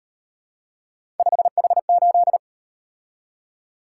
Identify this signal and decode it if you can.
Morse code, heard as a keyed tone, about 36 wpm, 700 Hz. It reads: 5H8